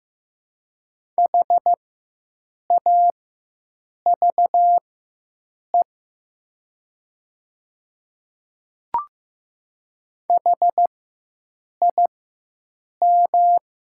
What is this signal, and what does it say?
Morse code, heard as a keyed tone, about 15 wpm, spaced as for 7 wpm, 700 Hz. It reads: HAVE HIM
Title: 4x Character Spacing - Top 100 Words - 15wpm